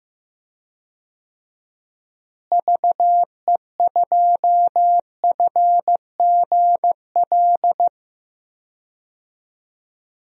Morse code, heard as a keyed tone, about 15 wpm, 700 Hz. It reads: VE2FGL
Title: DXCC-100 Call Signs - 6 Characters - Repeated 3x - 15wpm